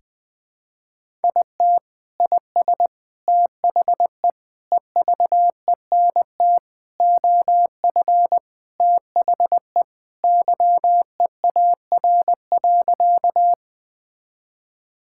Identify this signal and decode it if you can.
Morse code, heard as a keyed tone, about 21 wpm, 700 Hz. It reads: IT IS THE EVENT OF THE YEAR.
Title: Sentences with 3-Word Phrases - Spoken Beginning - 20wpm